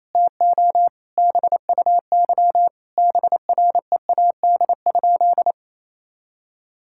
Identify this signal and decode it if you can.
Morse code, heard as a keyed tone, about 28 wpm, 700 Hz. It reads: TO BUY BREAD?
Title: Sentences from Top 1600 Words 28wpm